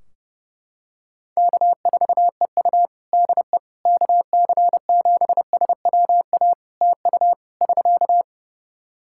Morse code, heard as a keyed tone, about 30 wpm, 700 Hz. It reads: K4EU DE KC7SWA TU <SK>
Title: Practice QSO - Exchange 4 - Ending - 30wpm